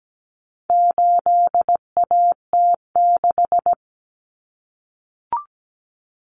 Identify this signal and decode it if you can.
Morse code, heard as a keyed tone, about 17 wpm, 700 Hz. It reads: 8AT6
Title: Common Cut Numbers - 4-Digit 17wpm